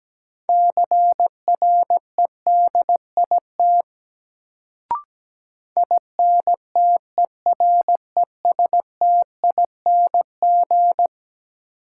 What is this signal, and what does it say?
Morse code, heard as a keyed tone, about 17 wpm, 700 Hz. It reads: CREDIT INTERESTING
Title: Sentences from Top 1000 Words - New Word Drill - 17wpm